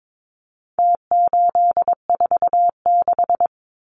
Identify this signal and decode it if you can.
Morse code, heard as a keyed tone, about 22 wpm, 700 Hz. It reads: T846
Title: Numbers - Contest Style - 4-Digit 22wpm